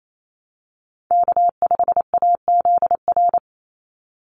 Morse code, heard as a keyed tone, about 28 wpm, 700 Hz. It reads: K5AZR
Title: Call Signs Repeated 2x - 28wpm